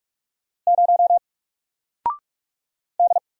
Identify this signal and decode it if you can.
Morse code, heard as a keyed tone, about 44 wpm, 700 Hz. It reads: 0 D
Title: Single Letter-Number Alt 45wpm